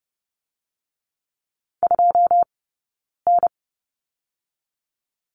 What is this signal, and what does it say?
Morse code, heard as a keyed tone, about 30 wpm, 700 Hz. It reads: 2 N
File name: 092 Lesson 16d - Letter c - Two Characters 30wpm